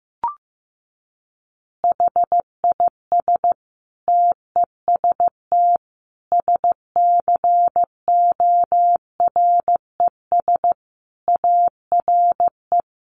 Morse code, heard as a keyed tone, about 15 wpm, 700 Hz. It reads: HIS TEST SCORES ARE
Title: Additional Easy Phrases 15wpm